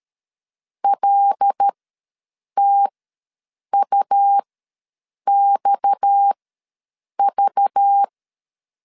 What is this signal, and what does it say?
Morse code, heard as a keyed tone, about 13 wpm, spaced as for 7 wpm, 800 Hz. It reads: LTUXV
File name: LEZIONI / 6 PKX0 / 6 PKX ws 12wpm 3106